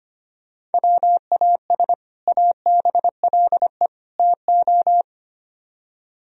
Morse code, heard as a keyed tone, about 31 wpm, 700 Hz. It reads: WAS ABLE TO
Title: Speed-Racing - Sets of 3 Words 25wpm